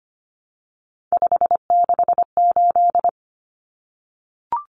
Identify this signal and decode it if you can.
Morse code, heard as a keyed tone, about 25 wpm, 700 Hz. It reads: 568
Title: Numbers - 3-Digit 25wpm